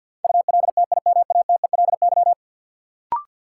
Morse code, heard as a keyed tone, about 52 wpm, 700 Hz. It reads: ULTIMATELY
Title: Sentences from Top 1200 Words - New Word Drill Alt - 50wpm